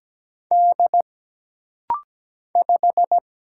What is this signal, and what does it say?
Morse code, heard as a keyed tone, about 17 wpm, 700 Hz. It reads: D 5
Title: Single Letter-Number - Rapid-Fire 17wpm